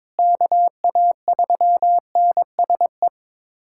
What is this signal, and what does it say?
Morse code, heard as a keyed tone, about 22 wpm, 700 Hz. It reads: KA3NSE
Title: USA Callsign DE Callsign - 6 Characters - 22wpm